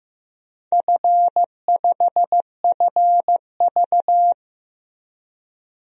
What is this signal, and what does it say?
Morse code, heard as a keyed tone, about 15 wpm, 700 Hz. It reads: F5FV
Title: DXCC-100 Call Signs - 4 Characters - 15wpm